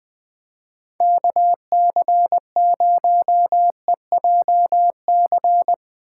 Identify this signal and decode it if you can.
Morse code, heard as a keyed tone, about 20 wpm, 700 Hz. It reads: KC0EJC